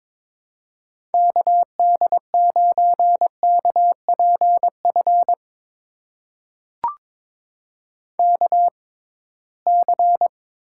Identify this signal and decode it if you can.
Morse code, heard as a keyed tone, about 23 wpm, spaced as for 17 wpm, 700 Hz. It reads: KD9KPF KC
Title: USA Call Signs - 6 Characters - 6x Character Spacing - 22wpm